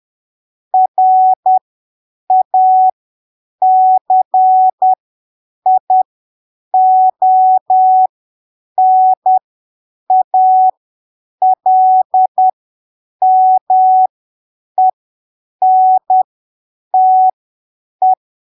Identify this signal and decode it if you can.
Morse code, heard as a keyed tone, about 10 wpm, spaced as for 7 wpm, 750 Hz. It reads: RACIONALMENTE
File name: arquivos / texto / texto 06 10ppm